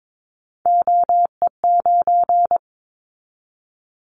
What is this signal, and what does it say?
Morse code, heard as a keyed tone, about 22 wpm, 700 Hz. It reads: OE9
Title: Callsign Prefixes for Top 50 DXCC - 22wpm